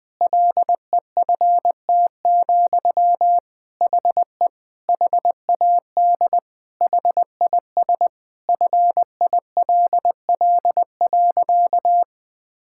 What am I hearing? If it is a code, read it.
Morse code, heard as a keyed tone, about 19 wpm, 700 Hz. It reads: LEFT, HE HAD HIS FILL.
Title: Sentences from Top 1000 Words - Review Part 5 - 20wpm